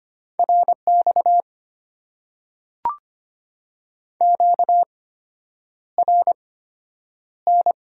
Morse code, heard as a keyed tone, about 25 wpm, spaced as for 9 wpm, 700 Hz. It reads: RX QRN
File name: Top 90 QSO Elements - 8x Character Spacing - 25wpm